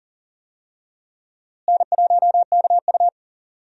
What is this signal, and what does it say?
Morse code, heard as a keyed tone, about 40 wpm, 700 Hz. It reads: N1KU
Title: USA Call Signs - 4 Characters - Repeated 2x - 40wpm